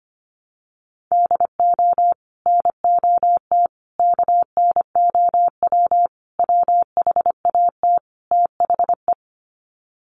Morse code, heard as a keyed tone, about 25 wpm, 700 Hz. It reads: DO NOT KNOW WHAT THE
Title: Sets of 5 Words 25wpm